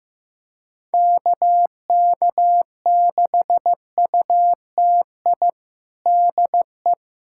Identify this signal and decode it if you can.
Morse code, heard as a keyed tone, about 15 wpm, 700 Hz. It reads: KK6UTI DE
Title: USA Callsign DE Callsign 15wpm